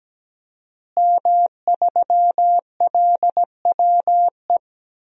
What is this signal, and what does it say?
Morse code, heard as a keyed tone, about 17 wpm, 700 Hz. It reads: M3LWE
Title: DXCC-100 Call Signs - 5 Characters - Repeated 2x - 17wpm